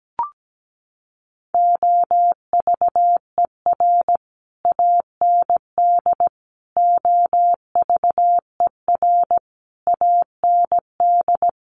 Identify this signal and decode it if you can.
Morse code, heard as a keyed tone, about 17 wpm, 700 Hz. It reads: OVER AND OVER AND
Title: Sets of 5 Words Encore 17wpm